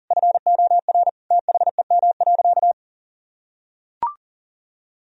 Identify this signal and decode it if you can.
Morse code, heard as a keyed tone, about 40 wpm, 700 Hz. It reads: FOR THEM.
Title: Sentences from Top 1000 Words - Review Part 4 - 40wpm